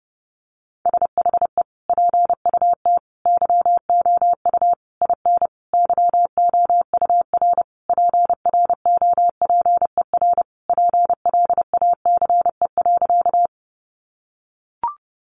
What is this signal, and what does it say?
Morse code, heard as a keyed tone, about 30 wpm, 700 Hz. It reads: SHE PUT YOU IN YOUR PROPER PLACE.
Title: Sentences from Top 800 Words 30wpm